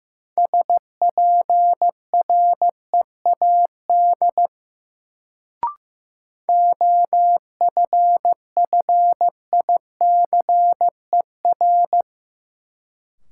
Morse code, heard as a keyed tone, about 15 wpm, 700 Hz. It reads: SPREAD OFFICER
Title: Sentences from Top 1200 Words - New Word Drill Alt - 15wpm